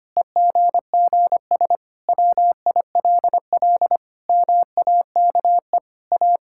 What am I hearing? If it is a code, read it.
Morse code, heard as a keyed tone, about 25 wpm, 700 Hz. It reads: EGGS WILL MAKE A
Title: Pangram Challenge v2 - 25wpm